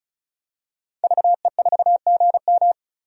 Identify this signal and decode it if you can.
Morse code, heard as a keyed tone, about 35 wpm, 700 Hz. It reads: VE4GM
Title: DXCC-100 Call Signs - 5 Characters - Repeated 3x - 35wpm